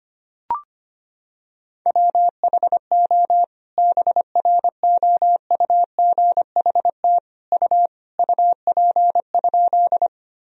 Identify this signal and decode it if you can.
Morse code, heard as a keyed tone, about 25 wpm, 700 Hz. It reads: WHO BROUGHT U UP?
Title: Sentences from Top 500 Words and Abbreviations 25wpm